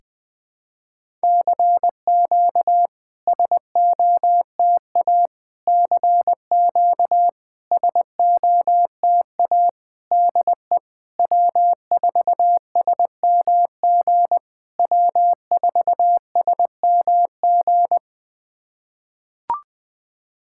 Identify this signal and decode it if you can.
Morse code, heard as a keyed tone, about 21 wpm, 700 Hz. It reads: CQ SOTA CQ SOTA DE W4SMG W4SMG
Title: SOTA QSO Practice 20wpm